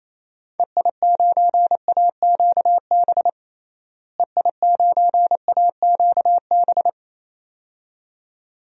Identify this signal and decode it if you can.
Morse code, heard as a keyed tone, about 28 wpm, 700 Hz. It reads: EI9AQB EI9AQB
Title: DXCC-100 Call Signs - 6 Characters - Repeated 2x - 28wpm